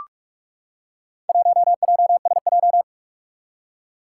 Morse code, heard as a keyed tone, about 43 wpm, 700 Hz. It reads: 1JSJ
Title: Four Challenging Letter-Number Combinations 45wpm